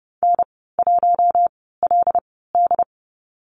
Morse code, heard as a keyed tone, about 30 wpm, spaced as for 17 wpm, 700 Hz. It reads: N1LD
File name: Lesson 14g - Number 2 - Call Signs 30wpm